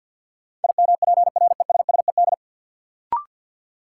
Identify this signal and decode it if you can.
Morse code, heard as a keyed tone, about 52 wpm, 700 Hz. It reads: IMPRESSED
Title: Sentences from Top 1600 Words - New Word Drill - 50wpm